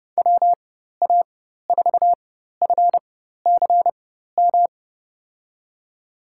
Morse code, heard as a keyed tone, about 30 wpm, spaced as for 14 wpm, 700 Hz. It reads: WA4FCM
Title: USA Call Signs - 6 Characters - 4x Character Spacing - 30wpm